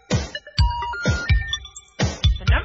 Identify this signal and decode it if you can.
Special information tone, with noise.